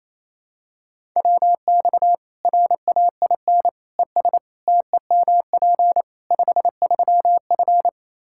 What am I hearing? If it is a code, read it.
Morse code, heard as a keyed tone, about 28 wpm, 700 Hz. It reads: WX RAIN ES TEMP 53F
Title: Practice QSO - Exchange 2 - Info - 28wpm